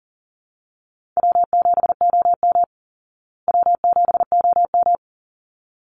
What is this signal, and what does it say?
Morse code, heard as a keyed tone, about 40 wpm, 700 Hz. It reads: W7OM W7OM
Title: Easy Call Signs Repeated 2x - 40wpm